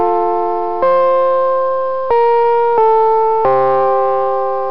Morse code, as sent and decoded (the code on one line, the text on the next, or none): none